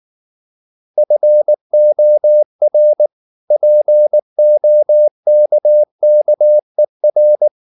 Morse code, heard as a keyed tone, about 19 wpm, 600 Hz. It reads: FOR POKKER
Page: Ferie 19 wpm